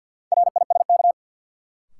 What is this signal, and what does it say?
Morse code, heard as a keyed tone, about 52 wpm, 700 Hz. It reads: RISK
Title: Sentences from Top 600 Words - New Word Drill - 50wpm